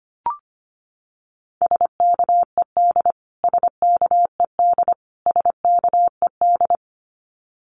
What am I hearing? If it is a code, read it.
Morse code, heard as a keyed tone, about 25 wpm, 700 Hz. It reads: SKED SKED SKED
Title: Abbreviations Repeated 3x - 25wpm